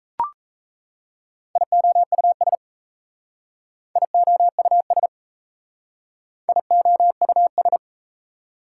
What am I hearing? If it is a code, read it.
Morse code, heard as a keyed tone, about 38 wpm, 700 Hz. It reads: IOUS IOUS IOUS